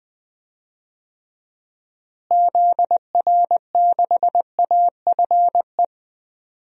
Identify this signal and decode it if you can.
Morse code, heard as a keyed tone, about 21 wpm, 700 Hz. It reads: ZR6AFE
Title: DXCC-100 Call Signs - 6 Characters - Repeated 2x - 20wpm